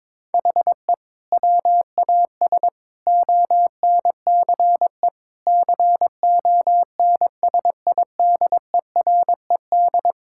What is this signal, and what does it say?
Morse code, heard as a keyed tone, about 22 wpm, 700 Hz. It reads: HE WAS ONCE CONSIDERED
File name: Sentences with 7-Word Phrases - 22wpm